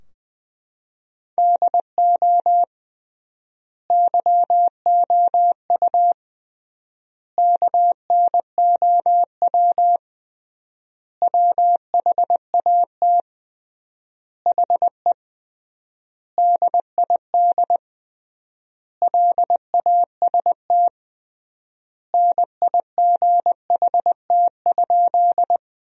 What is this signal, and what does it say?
Morse code, heard as a keyed tone, about 21 wpm, 700 Hz. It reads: DO YOU KNOW WHAT HE DID LAST NIGHT?
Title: 3x Word-Spacing - Sentences from Top 300 Words - 20wpm